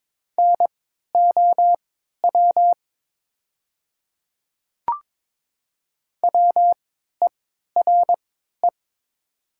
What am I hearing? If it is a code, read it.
Morse code, heard as a keyed tone, about 22 wpm, spaced as for 12 wpm, 700 Hz. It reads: NOW WERE
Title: 3x Character Spacing - Top 100 Words - 22wpm